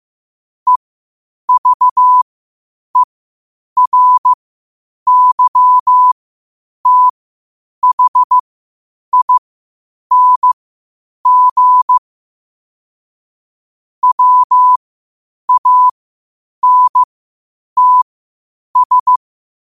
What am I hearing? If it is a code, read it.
Morse code, heard as a keyed tone, about 15 wpm, spaced as for 8 wpm, 1000 Hz. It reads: EVERYTHING WANTS
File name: qotdfiles / 8wpm